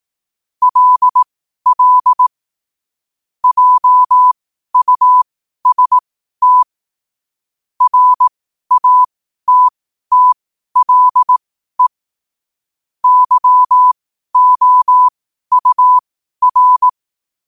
Morse code, heard as a keyed tone, about 18 wpm, spaced as for 12 wpm, 1000 Hz. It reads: LL JUST RATTLE YOUR
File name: qotdfiles / 12wpm